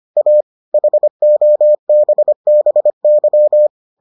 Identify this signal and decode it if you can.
Morse code, heard as a keyed tone, about 25 wpm, 600 Hz. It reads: A HOBBY